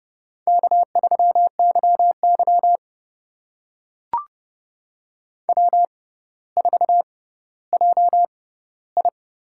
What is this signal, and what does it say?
Morse code, heard as a keyed tone, about 32 wpm, spaced as for 16 wpm, 700 Hz. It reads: K3YY W4JI
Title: USA Call Signs - 4 Characters - 6x Character Spacing - 30wpm